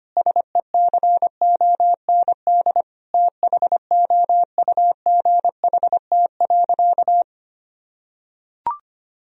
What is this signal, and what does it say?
Morse code, heard as a keyed tone, about 25 wpm, 700 Hz. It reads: SECOND THOUGHT.
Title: Sentences from Top 1000 Words - Review Part 1 - 25wpm